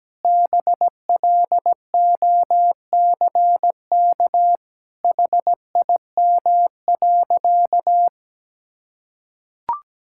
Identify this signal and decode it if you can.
Morse code, heard as a keyed tone, about 17 wpm, 700 Hz. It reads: BLOCK HIM.
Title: Sentences from Top 1600 Words 17wpm